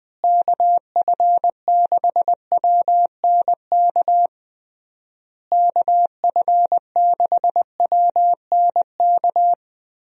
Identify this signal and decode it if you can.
Morse code, heard as a keyed tone, about 21 wpm, 700 Hz. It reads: KF6WNK KF6WNK
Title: USA Call Signs - 6 Characters - Repeated 3x - 20wpm